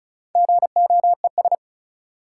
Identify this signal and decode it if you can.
Morse code, heard as a keyed tone, about 36 wpm, 700 Hz. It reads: GOES
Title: Sentences from Top 1500 Words - New Word Drill Alt - 35wpm